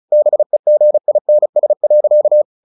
Morse code, heard as a keyed tone, about 34 wpm, 600 Hz. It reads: BEGINS.
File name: the old man and the sea 2 35wpm